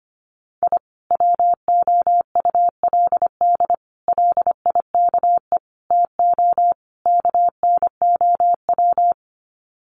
Morse code, heard as a keyed tone, about 25 wpm, 700 Hz. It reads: I WOULD LIKE TO KNOW
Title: Sets of 5 Words Encore 25wpm